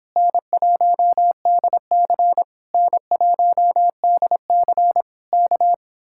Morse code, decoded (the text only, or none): N1DC N1DC K